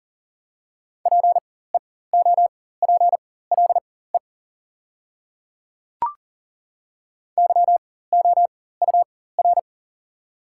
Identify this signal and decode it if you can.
Morse code, heard as a keyed tone, about 40 wpm, spaced as for 19 wpm, 700 Hz. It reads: PEOPLE YOUR
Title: 4x Character Spacing - Top 100 Words - 40wpm